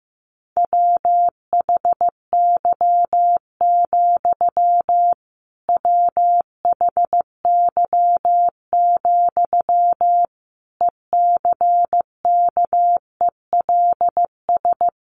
Morse code, heard as a keyed tone, about 14 wpm, 700 Hz. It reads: WHY, WHY, ECKELS
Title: A Sound Of Thunder 15wpm